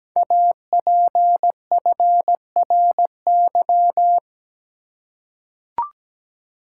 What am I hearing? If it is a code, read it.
Morse code, heard as a keyed tone, about 17 wpm, 700 Hz. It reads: APFRY
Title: Code Groups - 5 Letters - 17wpm